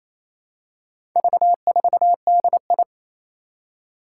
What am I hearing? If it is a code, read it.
Morse code, heard as a keyed tone, about 28 wpm, 700 Hz. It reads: V4DI
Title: Four Challenging Letter-Number Combinations 28wpm